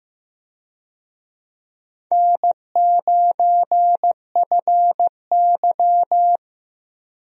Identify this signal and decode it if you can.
Morse code, heard as a keyed tone, about 15 wpm, 700 Hz. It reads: N9FY
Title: USA Call Signs - 4 Characters - Repeated 2x - 15wpm